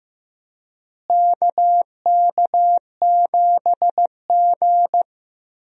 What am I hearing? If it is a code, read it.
Morse code, heard as a keyed tone, about 15 wpm, 700 Hz. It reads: KK7G
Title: USA Call Signs - 4 Characters - Repeated 3x - 15wpm